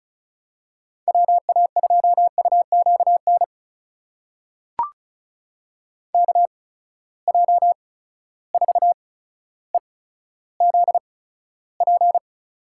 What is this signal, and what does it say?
Morse code, heard as a keyed tone, about 34 wpm, spaced as for 15 wpm, 700 Hz. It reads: WA2UQN KJ4EZP
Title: USA Call Signs - 8x Character Spacing - 35wpm